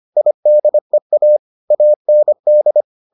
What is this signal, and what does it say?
Morse code, heard as a keyed tone, about 25 wpm, 600 Hz. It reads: IDEA AND